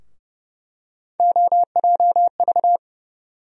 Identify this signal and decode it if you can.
Morse code, heard as a keyed tone, about 30 wpm, 700 Hz. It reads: OJV
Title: USA Call Sign Postfixes - 3 Letters 30wpm